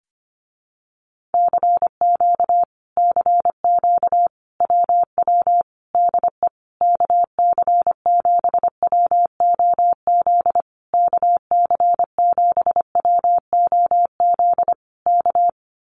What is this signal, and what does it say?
Morse code, heard as a keyed tone, about 25 wpm, 700 Hz. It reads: CQ CQ WW DE KC7WOZ KC7WOZ K
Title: Common QSO Phrases Encore 25wpm